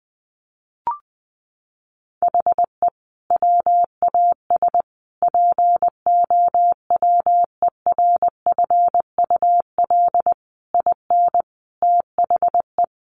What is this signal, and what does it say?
Morse code, heard as a keyed tone, about 20 wpm, 700 Hz. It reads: HE WAS POWERFUL IN THE
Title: Sentences from Top 1300 Words 20wpm